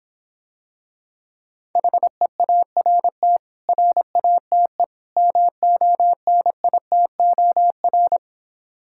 Morse code, heard as a keyed tone, about 26 wpm, 700 Hz. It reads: HEART RATE MONITOR